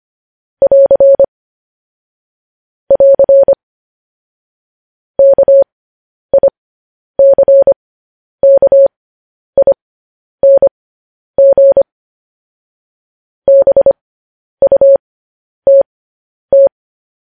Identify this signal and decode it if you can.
Morse code, heard as a keyed tone, about 25 wpm, spaced as for 10 wpm, 550 Hz. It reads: <AR> <AR> KICKING BUTT
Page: news headlines in morse code at 10 wpm